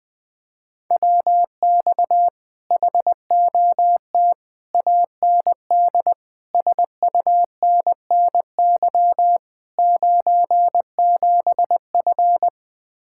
Morse code, heard as a keyed tone, about 21 wpm, 700 Hz. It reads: WX HOT AND SUNNY 97F
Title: Speed-Racing Common QSO Phrases - 20wpm